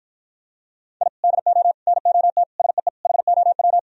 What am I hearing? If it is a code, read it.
Morse code, heard as a keyed tone, about 52 wpm, 700 Hz. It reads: I DO NOT SEE HOW